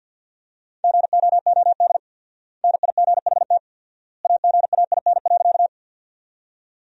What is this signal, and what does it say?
Morse code, heard as a keyed tone, about 50 wpm, 700 Hz. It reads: GOOD NIGHT AGAIN.
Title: 4x Word-Spacing - Sentences from Top 300 Words - 50wpm